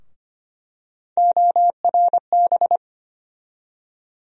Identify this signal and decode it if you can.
Morse code, heard as a keyed tone, about 25 wpm, 700 Hz. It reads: ORB
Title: USA Call Sign Postfixes - 3 Letters 25wpm